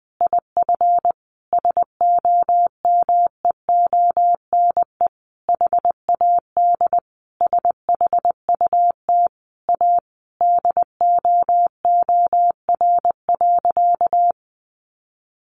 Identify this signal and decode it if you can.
Morse code, heard as a keyed tone, about 20 wpm, 700 Hz. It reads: IF SOMEONE HAD SHUT A DOOR.
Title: A Sound Of Thunder 20wpm